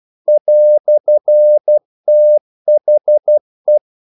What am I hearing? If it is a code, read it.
Morse code, heard as a keyed tone, about 12 wpm, 600 Hz. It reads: "THE
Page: English quotations in CW (telegraphy - morse)